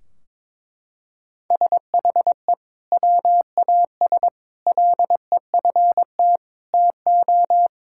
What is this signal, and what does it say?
Morse code, heard as a keyed tone, about 21 wpm, 700 Hz. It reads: SHE WAS LEFT TO
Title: Sentences from Top 1600 Words 22wpm